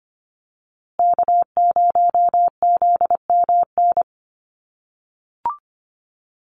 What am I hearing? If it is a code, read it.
Morse code, heard as a keyed tone, about 25 wpm, 700 Hz. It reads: K0ZMN
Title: USA Call Signs Repeated 2x 25wpm